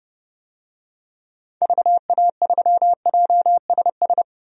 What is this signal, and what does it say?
Morse code, heard as a keyed tone, about 30 wpm, 700 Hz. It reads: VA3JSS
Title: DXCC-100 Call Signs - 6 Characters - 30wpm